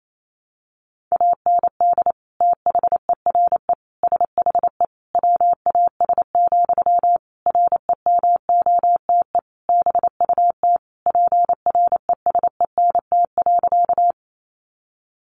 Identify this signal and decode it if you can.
Morse code, heard as a keyed tone, about 28 wpm, 700 Hz. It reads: AND THERE SHE WAS, REMOTE BUT PRESENT.
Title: Sentences from Top 1300 Words 28wpm